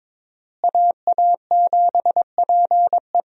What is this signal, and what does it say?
Morse code, heard as a keyed tone, about 22 wpm, 700 Hz. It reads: AA7PE
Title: USA Call Signs - 5 Characters - Repeated 3x - 22wpm